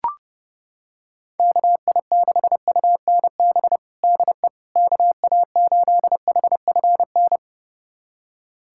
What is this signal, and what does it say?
Morse code, heard as a keyed tone, about 30 wpm, 700 Hz. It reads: KI6UNB DE KA8HFN
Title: USA Callsign DE Callsign - 6 Characters - 30wpm